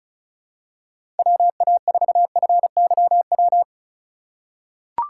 Morse code, heard as a keyed tone, about 34 wpm, 700 Hz. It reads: WA4FYW
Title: USA Call Signs - 6 Characters - 4x Character Spacing - 35wpm